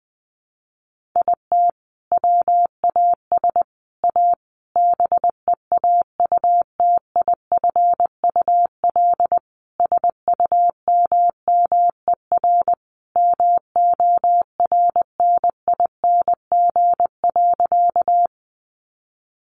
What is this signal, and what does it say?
Morse code, heard as a keyed tone, about 20 wpm, 700 Hz. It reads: IT WAS A BEAUTIFUL SUMMER MORNING.
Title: Sentences from Top 1100 Words 20wpm